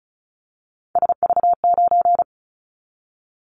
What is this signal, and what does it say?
Morse code, heard as a keyed tone, about 34 wpm, 700 Hz. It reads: SV9
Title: Callsign Prefixes for Top 100 DXCC - 35wpm